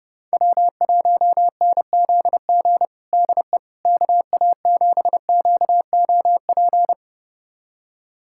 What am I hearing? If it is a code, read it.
Morse code, heard as a keyed tone, about 30 wpm, 700 Hz. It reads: W1NZG DE KA7QOP
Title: USA Callsign DE Callsign 30wpm